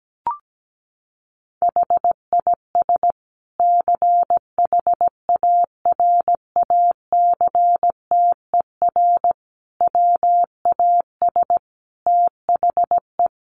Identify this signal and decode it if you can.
Morse code, heard as a keyed tone, about 17 wpm, 700 Hz. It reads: HIS CHARACTER WAS THE
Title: Sentences from Top 1300 Words 17wpm